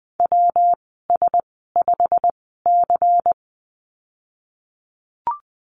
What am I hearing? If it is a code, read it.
Morse code, heard as a keyed tone, about 20 wpm, spaced as for 14 wpm, 700 Hz. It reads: WS5C
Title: 128 Lesson 21g - Number 6 - Call Signs 20wpm